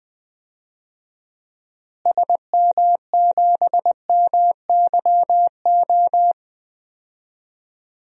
Morse code, heard as a keyed tone, about 21 wpm, 700 Hz. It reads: SM7MYO
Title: DXCC-100 Call Signs - 6 Characters - Repeated 2x - 20wpm